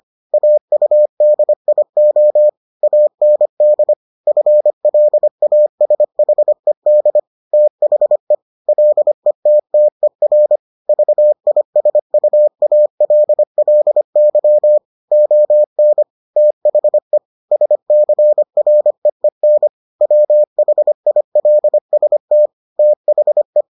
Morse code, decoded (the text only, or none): AUDIO AND FLASHED THE LETTER VISUALLY ON THE SCREEN WHILST THE